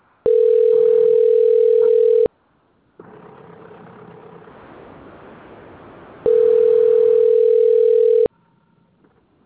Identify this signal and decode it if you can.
Ringback tone (North America), 440 + 480 Hz.